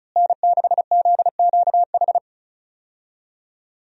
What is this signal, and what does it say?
Morse code, heard as a keyed tone, about 34 wpm, 700 Hz. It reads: N6ZQH
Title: USA Call Signs - 5 Characters - Repeated 2x - 35wpm